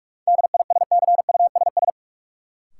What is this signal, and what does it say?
Morse code, heard as a keyed tone, about 42 wpm, 700 Hz. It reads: DISCUSS
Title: Sentences from Top 1000 Words - New Word Drill - 45wpm